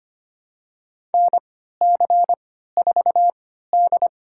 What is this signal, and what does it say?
Morse code, heard as a keyed tone, about 25 wpm, spaced as for 14 wpm, 700 Hz. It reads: NC4D
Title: USA Call Signs - 4 Characters - 3x Character Spacing - 25wpm